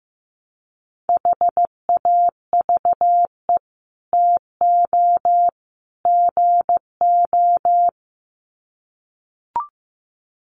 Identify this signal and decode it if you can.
Morse code, heard as a keyed tone, about 15 wpm, 700 Hz. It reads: HAVE TO GO
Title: Sets of 3 Words 15wpm